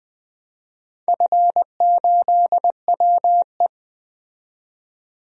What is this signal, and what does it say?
Morse code, heard as a keyed tone, about 20 wpm, 700 Hz. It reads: F8WE
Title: DXCC-100 Call Signs - 4 Characters - 20wpm